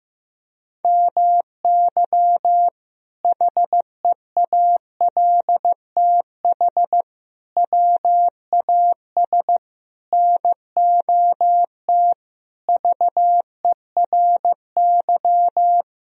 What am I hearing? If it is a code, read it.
Morse code, heard as a keyed tone, about 15 wpm, 700 Hz. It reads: MY HEALTH WAS NOT VERY
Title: Sentences from Top 1000 Words - Review Part 3 - 15wpm